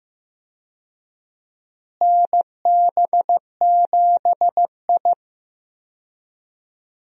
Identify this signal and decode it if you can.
Morse code, heard as a keyed tone, about 15 wpm, 700 Hz. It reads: NB7I
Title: USA Call Signs - 4 Characters - Repeated 3x - 15wpm